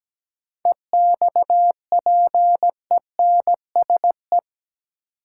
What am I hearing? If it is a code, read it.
Morse code, heard as a keyed tone, about 17 wpm, 700 Hz. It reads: EXPENSE
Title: Sets of 4 Words - New Word Drill - 17wpm